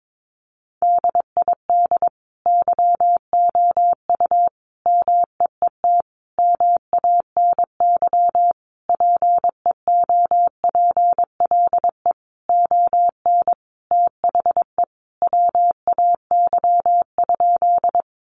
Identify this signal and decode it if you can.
Morse code, heard as a keyed tone, about 22 wpm, 700 Hz. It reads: DID YOU MEET MANY PEOPLE ON THE WAY?
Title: Sentences from Top 1100 Words 22wpm